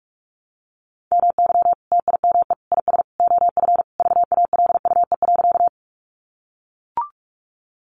Morse code, heard as a keyed tone, about 44 wpm, 700 Hz. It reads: MY TIME IS OF VALUE.
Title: Sentences from Top 600 Words 45wpm